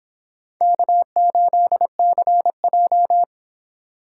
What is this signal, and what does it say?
Morse code, heard as a keyed tone, about 29 wpm, 700 Hz. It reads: K8CJ